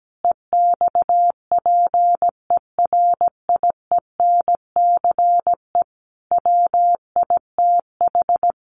Morse code, heard as a keyed tone, about 17 wpm, 700 Hz. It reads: EXPERIENCE WITH
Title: Sentences from Top 1400 Words 17wpm